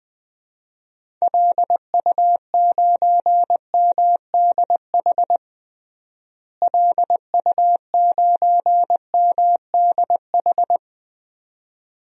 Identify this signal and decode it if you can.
Morse code, heard as a keyed tone, about 20 wpm, 700 Hz. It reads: LU9MDH LU9MDH
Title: DXCC-100 Call Signs - 6 Characters - Repeated 2x - 20wpm